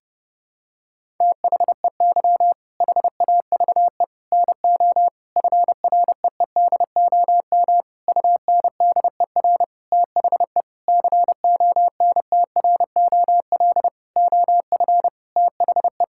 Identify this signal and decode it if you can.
Morse code, heard as a keyed tone, about 30 wpm, 700 Hz. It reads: THEY HAVE NO FREEDOM UNDER THE CONTROL OF THE
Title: Sentences from Top 1000 Words - Review Part 5 - 30wpm